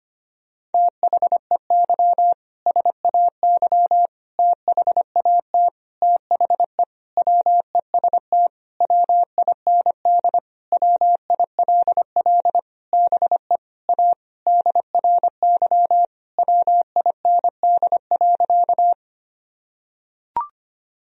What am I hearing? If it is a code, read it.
Morse code, heard as a keyed tone, about 25 wpm, 700 Hz. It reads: THEY SAY THAT THE WEST WIND WILL BE A DRY WIND.
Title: Sentences from Top 1000 Words - Review Part 3 - 25wpm